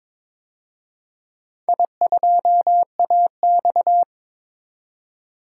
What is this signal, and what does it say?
Morse code, heard as a keyed tone, about 22 wpm, 700 Hz. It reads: I2AX
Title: DXCC-100 Call Signs - 4 Characters - 22wpm